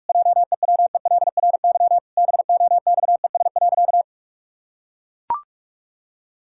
Morse code, heard as a keyed tone, about 41 wpm, 700 Hz. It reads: JEWELRY BOXES.
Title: Pangram Challenge v2 - 45wpm